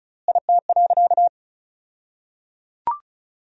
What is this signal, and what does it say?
Morse code, heard as a keyed tone, about 36 wpm, 700 Hz. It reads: IT.
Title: 2x Word-Spacing - Sentences from Top 100 Words - 35wpm